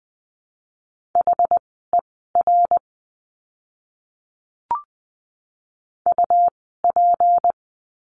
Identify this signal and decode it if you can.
Morse code, heard as a keyed tone, about 20 wpm, spaced as for 15 wpm, 700 Hz. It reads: HER UP
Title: 199 Lesson 32f - Letter k - Words 20wpm